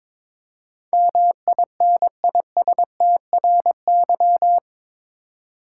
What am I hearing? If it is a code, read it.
Morse code, heard as a keyed tone, about 22 wpm, 700 Hz. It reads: MINISTRY
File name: practice / 22wpm / Eight Letter Words - 22wpm